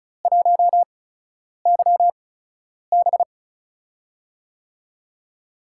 Morse code, heard as a keyed tone, about 36 wpm, spaced as for 10 wpm, 700 Hz. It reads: 1YB
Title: USA Call Signs - 4 Characters - 8x Character Spacing - 35wpm